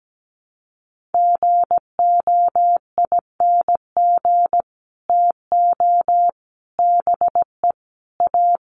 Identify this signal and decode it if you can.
Morse code, heard as a keyed tone, about 17 wpm, 700 Hz. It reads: GOING TO BE A
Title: Sets of 5 Words Encore 17wpm